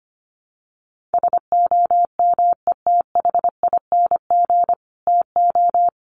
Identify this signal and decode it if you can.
Morse code, heard as a keyed tone, about 25 wpm, 700 Hz. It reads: SOMETHING TO
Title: Sentences from Top 1500 Words 25wpm